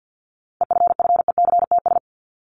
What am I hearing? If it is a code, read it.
Morse code, heard as a keyed tone, about 48 wpm, 700 Hz. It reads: EFFECTS
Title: ES:S Suffix - Top 100 Words with Base Words 50wpm